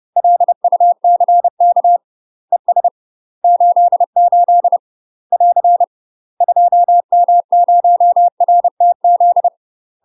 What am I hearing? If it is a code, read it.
Morse code, heard as a keyed tone, about 30 wpm, 700 Hz. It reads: LUCK ES 88 <AR> 2M0RTZ
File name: UK MASTER QSOs TRAINING / 30WPM / UK MASTER QSOs 0002